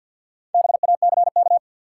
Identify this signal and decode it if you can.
Morse code, heard as a keyed tone, about 53 wpm, 700 Hz. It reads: BACK
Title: Sets of 2 Words - New Word Drill - 50wpm